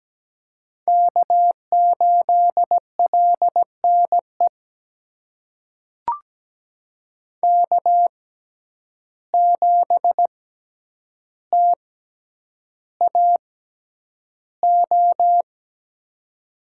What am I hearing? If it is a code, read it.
Morse code, heard as a keyed tone, about 17 wpm, spaced as for 9 wpm, 700 Hz. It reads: K8LNE K7TAO
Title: USA Call Signs - 5 Characters - 6x Character Spacing - 17wpm